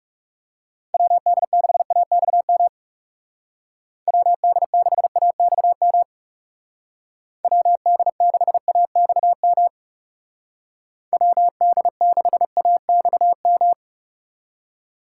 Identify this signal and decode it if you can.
Morse code, heard as a keyed tone, about 38 wpm, 700 Hz. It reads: WD6AXM WD6AXM WD6AXM WD6AXM